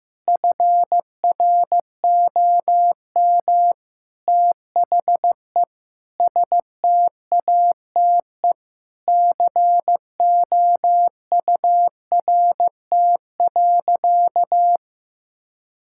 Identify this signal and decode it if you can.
Morse code, heard as a keyed tone, about 15 wpm, 700 Hz. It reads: FROM THE STATE COURT.
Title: Sentences from Top 1600 Words 15wpm